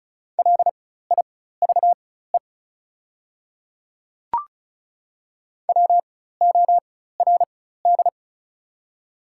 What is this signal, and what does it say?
Morse code, heard as a keyed tone, about 36 wpm, spaced as for 16 wpm, 700 Hz. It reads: LIVE WORD
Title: 4x Character Spacing - Top 200 Words - 35wpm